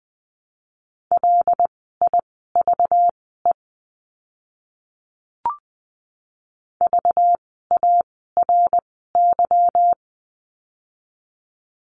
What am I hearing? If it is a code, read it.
Morse code, heard as a keyed tone, about 20 wpm, spaced as for 15 wpm, 700 Hz. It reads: LIVE VARY
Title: 192 Lesson 31f - Letter v - Words 20wpm